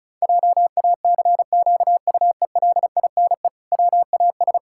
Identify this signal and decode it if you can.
Morse code, heard as a keyed tone, about 34 wpm, 700 Hz. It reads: JACQUELINE WAS
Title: Pangram Challenge v2 - 35wpm